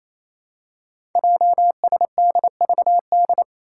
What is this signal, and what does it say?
Morse code, heard as a keyed tone, about 27 wpm, 700 Hz. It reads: JSDVD
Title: Code Groups - 5 Letters - 28wpm